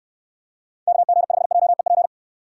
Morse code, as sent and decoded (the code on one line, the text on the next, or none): -.- --. ..... .--- . .--.
KG5JEP